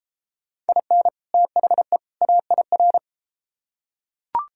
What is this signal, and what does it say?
Morse code, heard as a keyed tone, about 33 wpm, 700 Hz. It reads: IN THE AIR